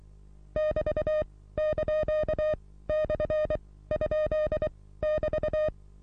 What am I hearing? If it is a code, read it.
Morse code, heard as a keyed tone, about 23 wpm, spaced as for 16 wpm, 620 Hz. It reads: <BT>)/?<BT>
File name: repos / Morsekurs / DJ4UF Morsen27